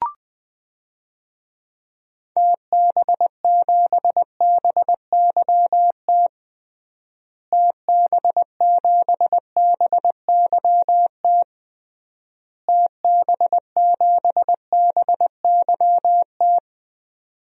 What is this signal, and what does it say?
Morse code, heard as a keyed tone, about 20 wpm, 700 Hz. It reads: TB7BYT TB7BYT TB7BYT
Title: DXCC-100 Call Signs - 6 Characters - Repeated 3x - 20wpm